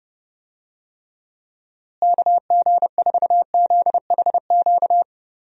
Morse code, heard as a keyed tone, about 32 wpm, 700 Hz. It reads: KG4ZHQ
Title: USA Call Signs - 6 Characters - Repeated 3x - 30wpm